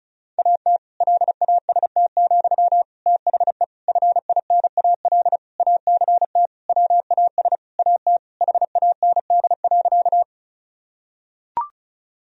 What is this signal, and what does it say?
Morse code, heard as a keyed tone, about 36 wpm, 700 Hz. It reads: AT LAST, THE FINAL ACT WAS AT HAND.
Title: Sentences from Top 1000 Words - Review Part 2 - 35wpm